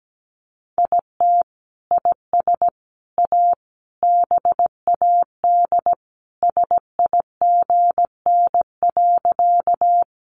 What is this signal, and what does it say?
Morse code, heard as a keyed tone, about 17 wpm, 700 Hz. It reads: IT IS A BAD SIGN.
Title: Sentences from Top 1000 Words 17wpm